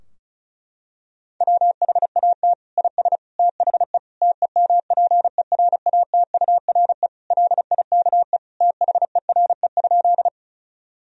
Morse code, heard as a keyed tone, about 36 wpm, 700 Hz. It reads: WHAT IS THE TEMPERATURE LIKE THERE?
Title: Sentences from Top 1000 Words - Review Part 4 - 35wpm